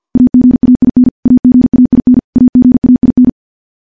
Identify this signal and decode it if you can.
Morse code, heard as a keyed tone, about 50 wpm, 250 Hz. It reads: AGAIN AGAIN AGAIN
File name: morse / morse-training-50WPM